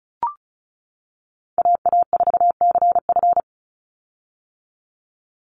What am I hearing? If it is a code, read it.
Morse code, heard as a keyed tone, about 34 wpm, 700 Hz. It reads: AA4CF
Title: Call Signs - 35wpm